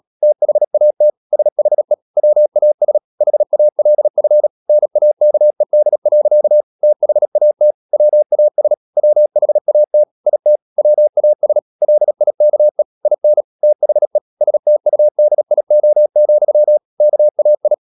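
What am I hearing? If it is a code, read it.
Morse code, heard as a keyed tone, about 37 wpm, 600 Hz. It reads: THAT SHE WAS HALF NAKED. THAT WAS WHAT IT WAS LIKE IN THE STUDIO, KAI
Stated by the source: Morse code practice at 35–39 wpm sending English text of Ken Follett's novel Never, from CW med Gnister.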